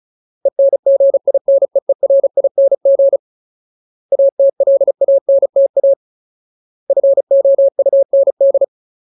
Morse code, heard as a keyed tone, about 34 wpm, 550 Hz. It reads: ENGINEERING ATLANTA FOUND
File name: Ultimate Word List / 35WPM Ultimate